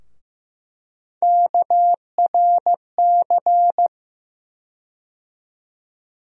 Morse code, heard as a keyed tone, about 15 wpm, 700 Hz. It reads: KRC
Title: USA Call Sign Postfixes - 3 Letters 15wpm